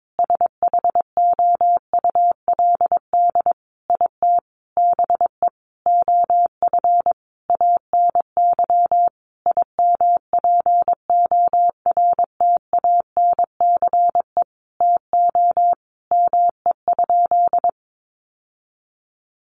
Morse code, heard as a keyed tone, about 22 wpm, 700 Hz. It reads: SHOULD IT BE OF ANY IMPORTANCE TO ME?
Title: Sentences from Top 1200 Words 22wpm